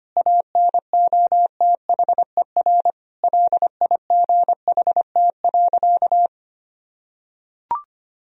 Morse code, heard as a keyed tone, about 25 wpm, 700 Hz. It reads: ANOTHER LIGHT.
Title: Sentences from Top 1000 Words - Review Part 1 - 25wpm